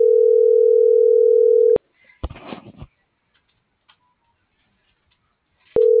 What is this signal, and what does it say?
Ringback tone (North America), 440 + 480 Hz.